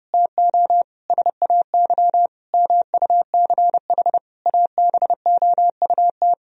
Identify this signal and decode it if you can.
Morse code, heard as a keyed tone, about 30 wpm, 700 Hz. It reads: TO SAY MUCH ABOUT
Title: Sentences from Top 1000 Words - Review Part 2 - 30wpm